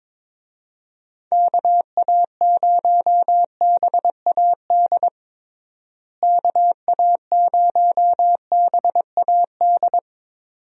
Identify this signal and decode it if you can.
Morse code, heard as a keyed tone, about 22 wpm, 700 Hz. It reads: KA0BAD KA0BAD
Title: USA Call Signs - 6 Characters - Repeated 2x - 22wpm